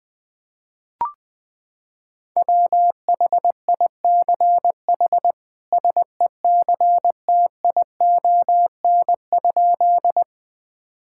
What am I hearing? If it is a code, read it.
Morse code, heard as a keyed tone, about 20 wpm, 700 Hz. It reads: WHICH SECTION?
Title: Sentences from Top 1000 Words - Review Part 5 - 20wpm